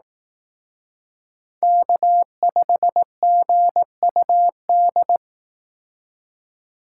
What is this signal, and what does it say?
Morse code, heard as a keyed tone, about 18 wpm, 700 Hz. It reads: K5GUD
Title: Speed Racing - USA 1x3 Call Signs 15wpm